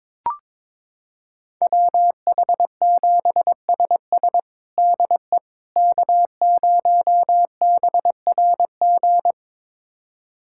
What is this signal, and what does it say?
Morse code, heard as a keyed tone, about 21 wpm, 700 Hz. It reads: WH7SS DE K0BRG
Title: USA Callsign DE Callsign - 5 Characters - 22wpm